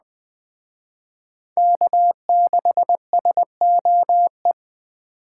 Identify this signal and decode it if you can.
Morse code, heard as a keyed tone, about 23 wpm, 700 Hz. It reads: K6SOE
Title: Speed Racing - USA 1x3 Call Signs 20wpm